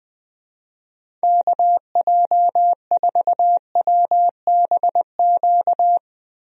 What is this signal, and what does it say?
Morse code, heard as a keyed tone, about 21 wpm, 700 Hz. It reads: KJ4WBQ
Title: USA Call Signs - 6 Characters - Repeated 2x - 20wpm